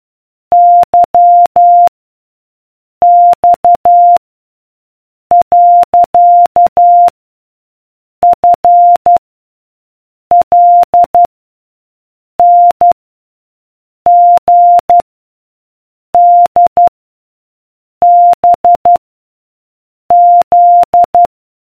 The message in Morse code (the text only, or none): Y X . F L N G D B Z